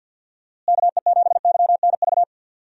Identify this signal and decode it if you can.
Morse code, heard as a keyed tone, about 52 wpm, 700 Hz. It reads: KE7YNV
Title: USA Call Signs - 6 Characters - Repeated 3x - 50wpm